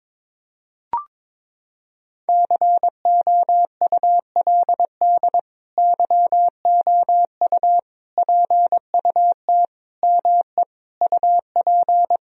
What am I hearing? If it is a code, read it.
Morse code, heard as a keyed tone, about 22 wpm, 700 Hz. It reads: COULD YOU PUT ME UP
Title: Sentences from Top 1000 Words - Review Part 3 - 22wpm